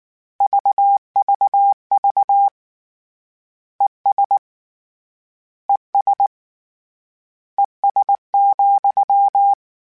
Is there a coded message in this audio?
Yes: Morse code from 0.4 to 9.5 s, heard as a keyed tone, about 19 wpm, 800 Hz. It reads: VVV ES ES ES,